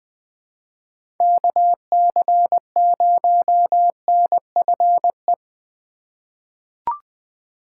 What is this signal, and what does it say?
Morse code, heard as a keyed tone, about 21 wpm, 700 Hz. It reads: KC0NFE
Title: USA Call Signs - 3x Character Spacing - 20wpm